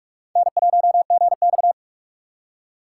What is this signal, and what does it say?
Morse code, heard as a keyed tone, about 46 wpm, 700 Hz. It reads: N1GX
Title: USA Call Signs - 3x Character Spacing - 45wpm